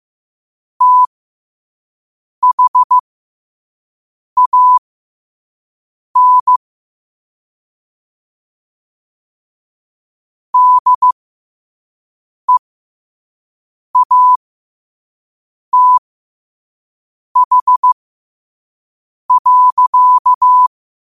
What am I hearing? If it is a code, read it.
Morse code, heard as a keyed tone, about 15 wpm, spaced as for 5 wpm, 1000 Hz. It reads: THAN DEATH.
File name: 5wpm